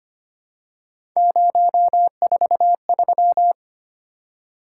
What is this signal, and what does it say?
Morse code, heard as a keyed tone, about 25 wpm, 700 Hz. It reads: 043
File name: Three Letter-Number Combinations - 25wpm